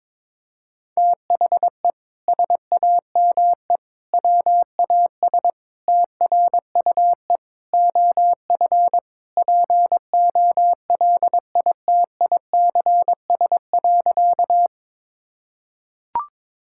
Morse code, heard as a keyed tone, about 22 wpm, 700 Hz. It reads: THE SAME WAS TRUE OF POLITICS.
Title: Sentences from Top 1000 Words - Review Part 6 - 22wpm